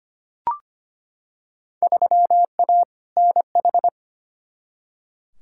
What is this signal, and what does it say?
Morse code, heard as a keyed tone, about 25 wpm, 700 Hz. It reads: 3A NH
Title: Field Day Exchanges - 25wpm